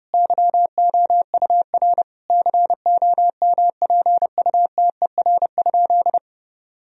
Morse code, heard as a keyed tone, about 32 wpm, 700 Hz. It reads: YOUR COMPUTER?
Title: Sentences from Top 1000 Words - Review Part 3 - 30wpm